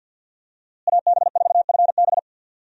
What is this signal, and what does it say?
Morse code, heard as a keyed tone, about 52 wpm, 700 Hz. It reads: AB4FB
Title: USA Call Signs - 5 Characters - Repeated 3x - 50wpm